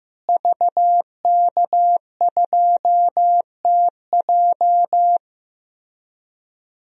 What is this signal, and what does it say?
Morse code, heard as a keyed tone, about 15 wpm, 700 Hz. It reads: VK2TJ
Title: DXCC-100 Call Signs - 5 Characters - Repeated 2x - 15wpm